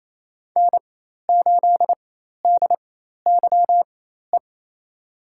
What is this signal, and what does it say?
Morse code, heard as a keyed tone, about 27 wpm, spaced as for 13 wpm, 700 Hz. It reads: N8DYE
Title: USA Call Signs - 5 Characters - 4x Character Spacing - 28wpm